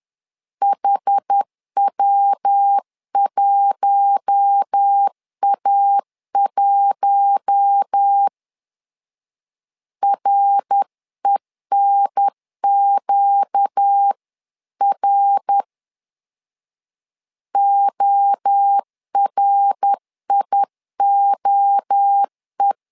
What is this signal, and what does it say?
Morse code, heard as a keyed tone, about 10 wpm, 800 Hz. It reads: HW1A1 RENQR ORIOE
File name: LEZIONI / 4 Q1 10wpm 2448